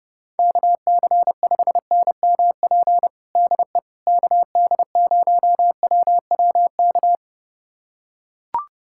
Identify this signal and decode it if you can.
Morse code, heard as a keyed tone, about 30 wpm, 700 Hz. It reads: KC5NMP DE KD0WWK
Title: USA Callsign DE Callsign 30wpm